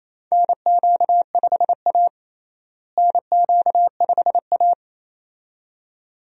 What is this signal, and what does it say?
Morse code, heard as a keyed tone, about 27 wpm, 700 Hz. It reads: NQ5A NQ5A
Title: USA Call Signs - 4 Characters - Repeated 2x - 28wpm